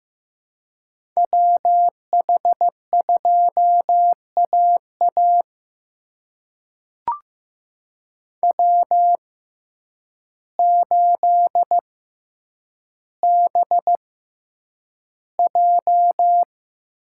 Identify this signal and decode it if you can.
Morse code, heard as a keyed tone, about 15 wpm, spaced as for 8 wpm, 700 Hz. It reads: WH2AA W8BJ
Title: USA Call Signs - 6x Character Spacing - 15wpm